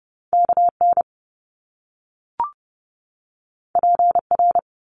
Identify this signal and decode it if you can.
Morse code, heard as a keyed tone, about 30 wpm, 700 Hz. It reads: KN PR
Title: Top Two Letter Combinations in Words 30wpm